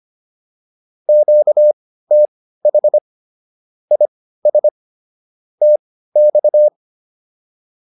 Morse code, heard as a keyed tone, about 25 wpm, spaced as for 15 wpm, 600 Hz. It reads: QTH IS TX